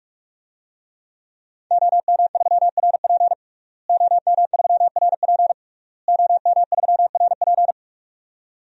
Morse code, heard as a keyed tone, about 46 wpm, 700 Hz. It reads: OM3RP OM3RP OM3RP
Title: DXCC-100 Call Signs - 5 Characters - Repeated 3x - 45wpm